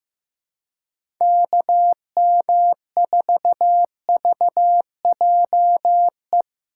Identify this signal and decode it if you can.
Morse code, heard as a keyed tone, about 15 wpm, 700 Hz. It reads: KM4VJE
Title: USA Call Signs - 4x Character Spacing - 15wpm